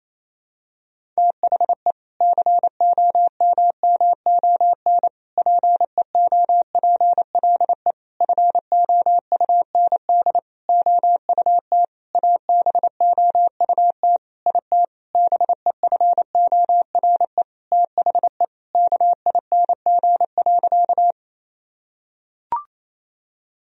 Morse code, heard as a keyed tone, about 28 wpm, 700 Hz. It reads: THE COMMON PEOPLE FOUND OUT ABOUT IT BEFORE THE KING.
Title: Sentences from Top 1000 Words - Review Part 3 - 28wpm